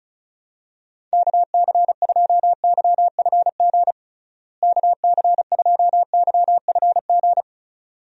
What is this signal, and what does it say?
Morse code, heard as a keyed tone, about 36 wpm, 700 Hz. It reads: KC2YFG KC2YFG
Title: USA Call Signs - 6 Characters - Repeated 2x - 35wpm